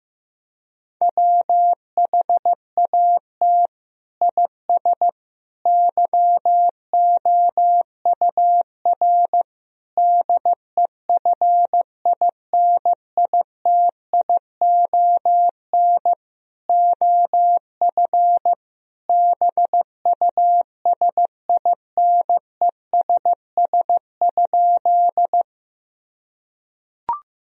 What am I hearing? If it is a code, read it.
Morse code, heard as a keyed tone, about 15 wpm, 700 Hz. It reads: WHAT IS YOUR DEFINITION OF BUSINESS?
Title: Sentences from Top 1000 Words - Review Part 4 - 15wpm